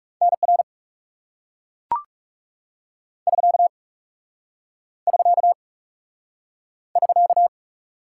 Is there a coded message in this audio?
Yes: Morse code from 0.2 to 7.5 s, heard as a keyed tone, about 38 wpm, 700 Hz. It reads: NR <SK> <SK> <SK>